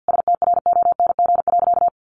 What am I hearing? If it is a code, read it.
Morse code, heard as a keyed tone, about 50 wpm, 700 Hz. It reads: STRONG.
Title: Sentences from Top 1400 Words 50wpm